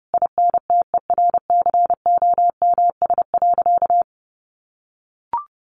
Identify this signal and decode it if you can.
Morse code, heard as a keyed tone, about 30 wpm, 700 Hz. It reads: INTERCOMS.
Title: A Sound Of Thunder 30wpm